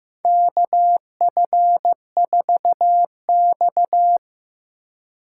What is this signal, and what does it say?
Morse code, heard as a keyed tone, about 15 wpm, 700 Hz. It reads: KF4X
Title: USA Call Signs - 4 Characters - 4x Character Spacing - 15wpm